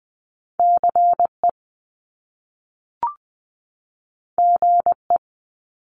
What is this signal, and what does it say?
Morse code, heard as a keyed tone, about 20 wpm, 700 Hz. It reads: CE GE
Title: Top Two Letter Combinations in Words 20wpm